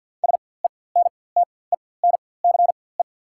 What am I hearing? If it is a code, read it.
Morse code, heard as a keyed tone, about 47 wpm, spaced as for 23 wpm, 690 Hz. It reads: SENTENCE
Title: 4x Character Spacing - Top 300 Words - 50wpm